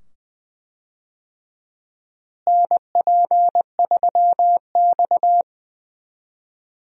Morse code, heard as a keyed tone, about 20 wpm, 700 Hz. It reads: NP3X
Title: USA Call Signs - 4 Characters - Repeated 2x - 20wpm